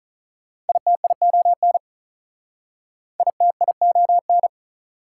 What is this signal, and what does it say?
Morse code, heard as a keyed tone, about 47 wpm, 700 Hz. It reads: ITION ITION